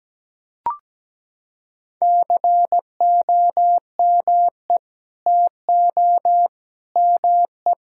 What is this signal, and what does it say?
Morse code, heard as a keyed tone, about 17 wpm, 700 Hz. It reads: COME TO ME
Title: Sentences from Top 1000 Words - Review Part 6 - 17wpm